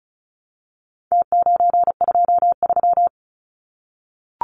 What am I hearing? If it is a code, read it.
Morse code, heard as a keyed tone, about 34 wpm, 700 Hz. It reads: T923
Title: Numbers - Contest Style - 4-Digit 35wpm